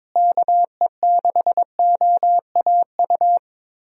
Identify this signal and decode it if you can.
Morse code, heard as a keyed tone, about 22 wpm, 700 Hz. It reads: KE6OAU
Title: USA Call Signs - 6 Characters - Repeated 3x - 22wpm